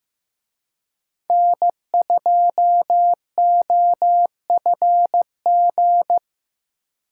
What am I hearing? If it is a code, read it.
Morse code, heard as a keyed tone, about 18 wpm, 700 Hz. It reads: N2OFG
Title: Speed Racing - USA Call Signs 15wpm